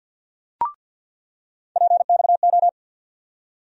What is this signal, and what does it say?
Morse code, heard as a keyed tone, about 46 wpm, 700 Hz. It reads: PXO
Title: USA Call Sign Postfixes - 3 Letters 50wpm